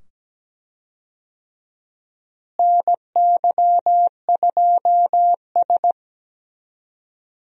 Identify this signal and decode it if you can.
Morse code, heard as a keyed tone, about 17 wpm, 700 Hz. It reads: NY2S
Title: USA Call Signs - 4 Characters - Repeated 2x - 17wpm